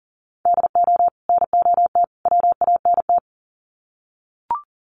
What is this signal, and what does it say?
Morse code, heard as a keyed tone, about 38 wpm, 700 Hz. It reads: DO NOT WANT
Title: Sets of 3 Words 40wpm